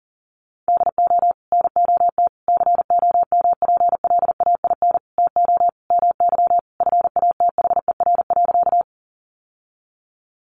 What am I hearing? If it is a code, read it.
Morse code, heard as a keyed tone, about 37 wpm, 700 Hz. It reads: DO NOT COMPLAIN TO MY FATHER.
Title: Sentences from Top 1300 Words 40wpm